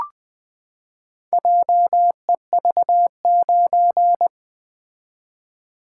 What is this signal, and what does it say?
Morse code, heard as a keyed tone, about 20 wpm, 700 Hz. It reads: JEV9
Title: Four Letter-Number Combinations - 20wpm